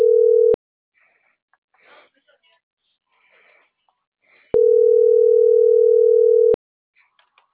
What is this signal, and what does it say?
Ringback tone (North America), 440 + 480 Hz.